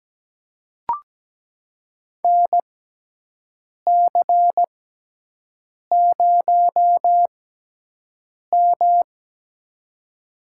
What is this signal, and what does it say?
Morse code, heard as a keyed tone, about 17 wpm, spaced as for 6 wpm, 700 Hz. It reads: NC0M
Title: USA Call Signs - 4 Characters - 6x Character Spacing - 17wpm